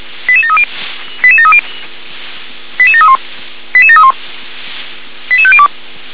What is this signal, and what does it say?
Five-tone selective call (ZVEI), heard as quick runs of five tones, tones of about 70 ms. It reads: 88730 88730 88731 88731 88502
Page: ZVEI